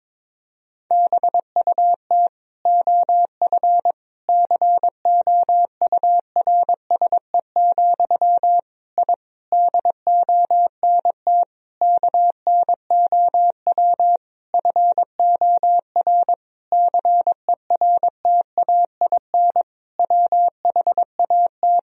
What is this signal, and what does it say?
Morse code, heard as a keyed tone, about 22 wpm, 700 Hz. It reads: BUT OF COURSE, I DONT KNOW FOR CERTAIN WHAT
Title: Sentences from Top 1000 Words - Review Part 2 - 22wpm